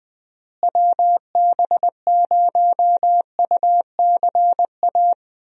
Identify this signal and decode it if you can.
Morse code, heard as a keyed tone, about 20 wpm, 700 Hz. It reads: WB0UCA
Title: USA Call Signs - 6 Characters - Repeated 2x - 20wpm